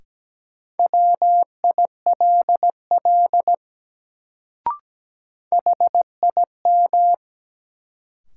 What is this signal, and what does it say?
Morse code, heard as a keyed tone, about 17 wpm, 700 Hz. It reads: WILL HIM
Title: Sets of 2 Words - New Word Drill - 17wpm